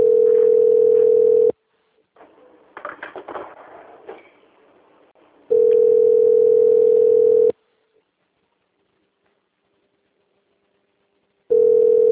Ringback tone (North America), 440 + 480 Hz.
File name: out- - - - - .28495